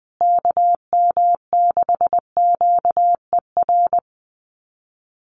Morse code, heard as a keyed tone, about 20 wpm, 700 Hz. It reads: KM6QER
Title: USA Call Signs Repeated 3x 20wpm